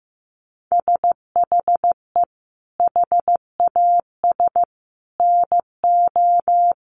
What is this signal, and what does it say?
Morse code, heard as a keyed tone, about 15 wpm, 700 Hz. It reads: SHE HAS NO
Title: Sentences from Top 1300 Words 15wpm